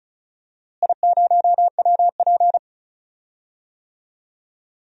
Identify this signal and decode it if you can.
Morse code, heard as a keyed tone, about 36 wpm, 700 Hz. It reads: I0WP
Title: DXCC-100 Call Signs - 4 Characters - Repeated 3x - 35wpm